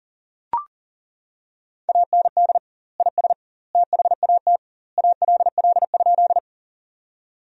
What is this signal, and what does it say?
Morse code, heard as a keyed tone, about 40 wpm, 700 Hz. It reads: AND IS THAT ALL?
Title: 2x Word-Spacing - Sentences from Top 100 Words - 40wpm